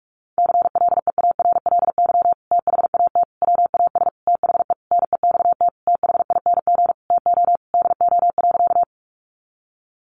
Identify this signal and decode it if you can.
Morse code, heard as a keyed tone, about 44 wpm, 700 Hz. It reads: CLEARLY THAT WAS THE NEXT THING TO DO.
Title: Sentences from Top 1300 Words 45wpm